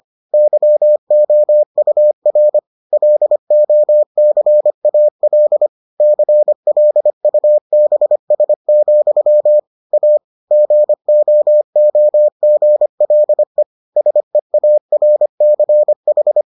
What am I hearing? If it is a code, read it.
Morse code, heard as a keyed tone, about 25 wpm, 600 Hz. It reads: YOUR LOCAL CLUBS, A GOOGLE SEARCH